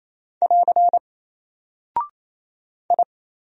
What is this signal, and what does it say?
Morse code, heard as a keyed tone, about 27 wpm, 700 Hz. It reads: <AR> I
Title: Single Character (Letter:Number:Punctuation:Prosign) 28wpm